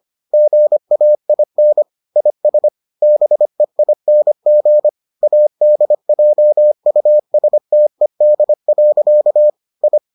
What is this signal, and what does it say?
Morse code, heard as a keyed tone, about 25 wpm, 600 Hz. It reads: GAIN IS BEING ADJUSTED. I